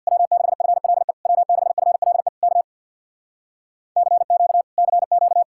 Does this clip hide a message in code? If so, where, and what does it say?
Yes: Morse code from 0.1 to 5.4 s, heard as a keyed tone, about 66 wpm, 700 Hz. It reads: W6FLE W6FLE K CQ CQ